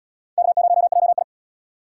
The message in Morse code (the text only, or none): K0JI